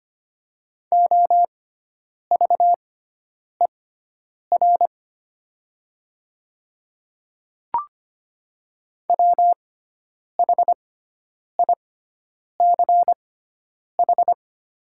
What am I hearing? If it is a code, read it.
Morse code, heard as a keyed tone, about 25 wpm, spaced as for 9 wpm, 700 Hz. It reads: OVER WHICH
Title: 6x Character Spacing - Top 100 Words - 25wpm